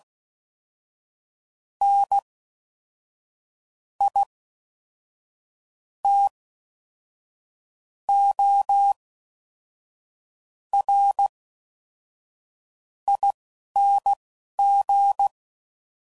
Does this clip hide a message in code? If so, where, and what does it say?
Yes: Morse code from 1.8 to 15.3 s, heard as a keyed tone, about 16 wpm, spaced as for 5 wpm, 800 Hz. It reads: NITORING